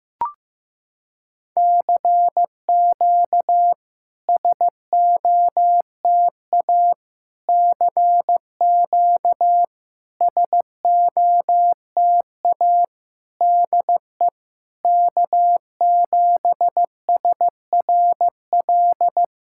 Morse code, heard as a keyed tone, about 15 wpm, 690 Hz. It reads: CQ SOTA CQ SOTA DE K7SRL
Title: SOTA QSO Practice 15wpm